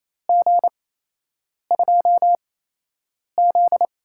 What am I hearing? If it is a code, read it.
Morse code, heard as a keyed tone, about 27 wpm, spaced as for 8 wpm, 700 Hz. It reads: G2Z
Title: USA Call Signs - 8x Character Spacing - 28wpm